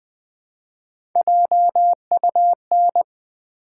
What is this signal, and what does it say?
Morse code, heard as a keyed tone, about 20 wpm, 700 Hz. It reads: JUN
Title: Build A Word - 1 Letter At A Time - 4-Letter Words - 20wpm